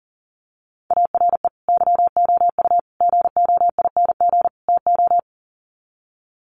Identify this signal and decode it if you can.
Morse code, heard as a keyed tone, about 40 wpm, 700 Hz. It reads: ARE YOU GOING TO
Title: Sets of 4 Words 40wpm